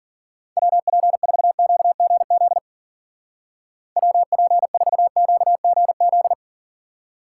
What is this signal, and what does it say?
Morse code, heard as a keyed tone, about 52 wpm, 700 Hz. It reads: WP4QGZ WP4QGZ